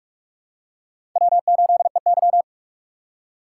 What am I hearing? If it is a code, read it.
Morse code, heard as a keyed tone, about 46 wpm, 700 Hz. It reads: W8EY
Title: USA Call Signs - 4 Characters - 4x Character Spacing - 45wpm